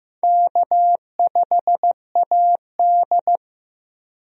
Morse code, heard as a keyed tone, about 15 wpm, 700 Hz. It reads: K5AD
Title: USA Call Signs - 4 Characters - Repeated 2x - 15wpm